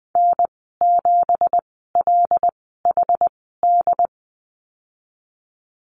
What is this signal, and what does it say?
Morse code, heard as a keyed tone, about 19 wpm, spaced as for 14 wpm, 700 Hz. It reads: N7LHD
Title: 193 Lesson 31g - Letter v - Call Signs 20wpm